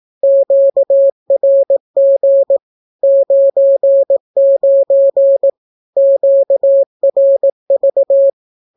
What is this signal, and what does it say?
Morse code, heard as a keyed tone, about 18 wpm, 550 Hz. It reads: QRG 99 QRV
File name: naukapodstaw / skroty / skroty